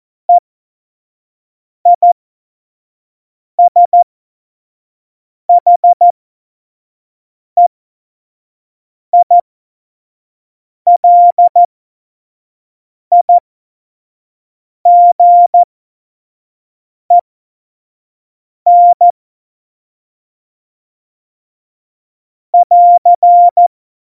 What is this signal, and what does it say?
Morse code, heard as a keyed tone, about 14 wpm, spaced as for 5 wpm, 700 Hz. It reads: EISHEILIGEN <AR>
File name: Morsekurs / Witze / Witz 26. Januar 2021, 25 F70